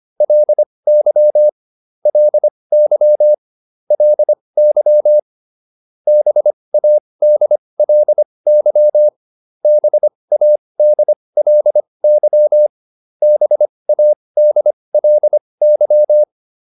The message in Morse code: LY LY LY BADLY BADLY BADLY

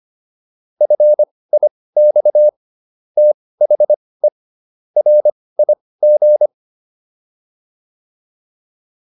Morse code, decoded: FIX THE RIG

